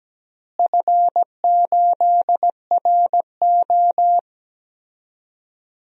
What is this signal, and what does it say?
Morse code, heard as a keyed tone, about 17 wpm, 700 Hz. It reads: F8RO
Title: DXCC-100 Call Signs - 4 Characters - Repeated 3x - 17wpm